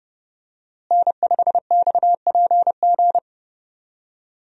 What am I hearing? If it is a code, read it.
Morse code, heard as a keyed tone, about 30 wpm, 700 Hz. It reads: N5XPG
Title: USA Call Signs - 5 Characters - Repeated 3x - 30wpm